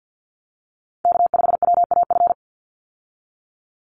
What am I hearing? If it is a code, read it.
Morse code, heard as a keyed tone, about 50 wpm, 700 Hz. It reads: K5WAF
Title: Call Signs Repeated 2x - 50wpm